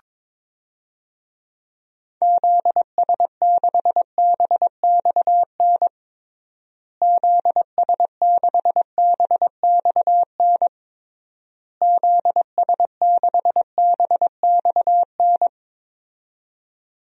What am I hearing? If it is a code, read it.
Morse code, heard as a keyed tone, about 22 wpm, 700 Hz. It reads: ZS6BXN ZS6BXN ZS6BXN
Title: DXCC-100 Call Signs - 6 Characters - Repeated 3x - 22wpm